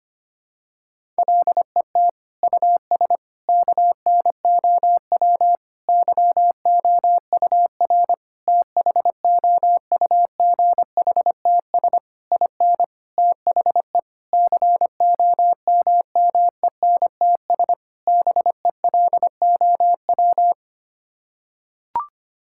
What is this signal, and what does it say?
Morse code, heard as a keyed tone, about 25 wpm, 700 Hz. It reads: LET US KNOW YOUR THOUGHTS IN THE COMMENTS BELOW
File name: Sets of 9 Words - 25wpm